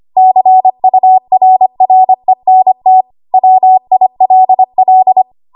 Morse code, heard as a keyed tone, about 25 wpm, 750 Hz. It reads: CURRENT WILL
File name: Archive / 25 WPM / 25WPM